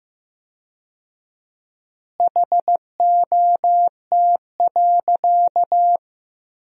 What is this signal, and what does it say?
Morse code, heard as a keyed tone, about 15 wpm, 700 Hz. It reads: HOT.
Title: 4x Word-Spacing - Sentences from Top 500 Words - 15wpm